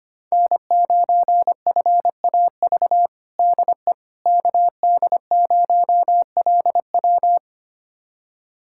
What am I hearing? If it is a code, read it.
Morse code, heard as a keyed tone, about 25 wpm, 700 Hz. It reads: N9FAV DE KD0LW
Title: USA Callsign DE Callsign - 5 Characters - 25wpm